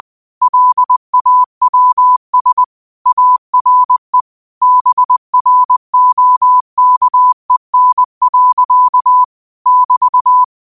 Morse code, heard as a keyed tone, about 20 wpm, 1000 Hz. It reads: LAWS ARE BROKEN. <BT>
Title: 20 WPM morse code quotes for Thu, 29 Feb 2024